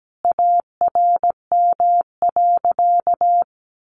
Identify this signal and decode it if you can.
Morse code, heard as a keyed tone, about 17 wpm, 700 Hz. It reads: ARM.
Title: A Sound Of Thunder 17wpm